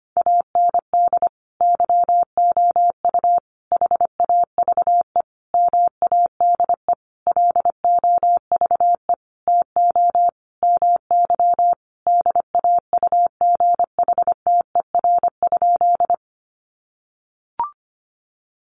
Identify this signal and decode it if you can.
Morse code, heard as a keyed tone, about 25 wpm, 700 Hz. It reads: AND YOU HAVE MADE LOVE TO MY DAUGHTER?
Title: Sentences from Top 1500 Words 25wpm